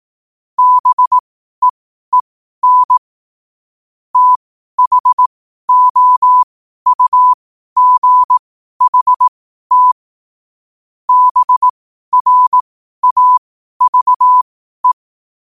Morse code, heard as a keyed tone, about 18 wpm, spaced as for 12 wpm, 1000 Hz. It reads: BEEN THOUGHT BRAVE
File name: qotdfiles / 12wpm